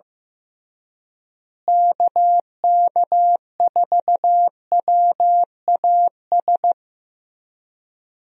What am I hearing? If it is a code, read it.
Morse code, heard as a keyed tone, about 15 wpm, 700 Hz. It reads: KK4WAS
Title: USA Call Signs - 6 Characters - Repeated 3x - 15wpm